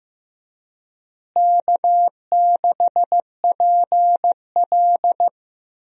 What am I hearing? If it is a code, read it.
Morse code, heard as a keyed tone, about 15 wpm, 700 Hz. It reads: K6PL
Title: USA Call Signs - 4 Characters - Repeated 2x - 15wpm